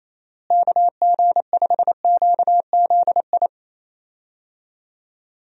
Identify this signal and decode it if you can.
Morse code, heard as a keyed tone, about 27 wpm, 700 Hz. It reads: KG5QZI
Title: USA Call Signs - 6 Characters - Repeated 3x - 28wpm